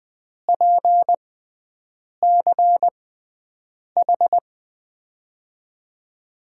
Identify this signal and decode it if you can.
Morse code, heard as a keyed tone, about 21 wpm, spaced as for 7 wpm, 700 Hz. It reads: PCH
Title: USA Call Signs - 6 Characters - 6x Character Spacing - 20wpm